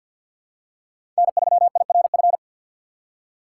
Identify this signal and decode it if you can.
Morse code, heard as a keyed tone, about 48 wpm, 700 Hz. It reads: N3IRF
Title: USA Call Signs - 5 Characters - 4x Character Spacing - 50wpm